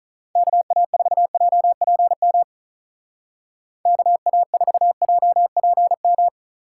Morse code, heard as a keyed tone, about 43 wpm, 700 Hz. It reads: KA4JPM KA4JPM
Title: Speed Racing - USA 2x3 Call Signs 35wpm